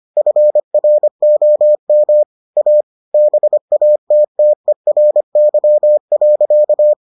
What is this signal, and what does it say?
Morse code, heard as a keyed tone, about 25 wpm, 600 Hz. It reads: FROM A BATTERY.